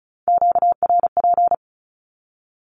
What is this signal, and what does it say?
Morse code, heard as a keyed tone, about 36 wpm, 700 Hz. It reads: QRP
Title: Q-Codes with Spoken Meaning - 35wpm